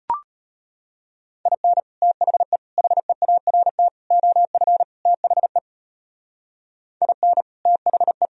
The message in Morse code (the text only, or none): IN THE HEART OF THE IN THE